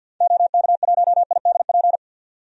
Morse code, heard as a keyed tone, about 52 wpm, 700 Hz. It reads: OK1IDP
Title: DXCC-100 Call Signs - 6 Characters - 50wpm